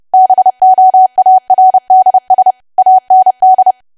Morse code, heard as a keyed tone, about 30 wpm, 750 Hz. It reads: BOARDS AND
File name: cwa / W1AW-Level3 / 30WPM